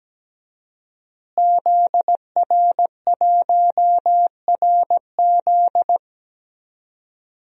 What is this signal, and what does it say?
Morse code, heard as a keyed tone, about 17 wpm, 700 Hz. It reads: ZR1RZ
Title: DXCC-100 Call Signs - 5 Characters - Repeated 3x - 17wpm